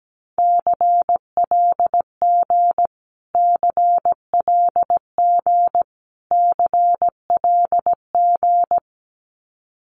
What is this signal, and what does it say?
Morse code, heard as a keyed tone, about 17 wpm, 700 Hz. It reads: CLG CLG CLG
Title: Abbreviations Repeated 3x - 17wpm